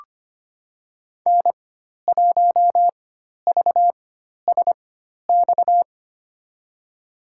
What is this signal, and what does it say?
Morse code, heard as a keyed tone, about 25 wpm, spaced as for 12 wpm, 700 Hz. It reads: N1VSX
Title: USA Call Signs - 5 Characters - 4x Character Spacing - 25wpm